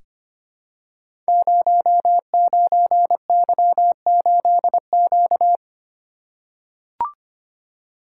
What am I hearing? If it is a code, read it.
Morse code, heard as a keyed tone, about 25 wpm, 700 Hz. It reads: 09Y8Q
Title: Code Groups - 5 Letter:Numbers - 25wpm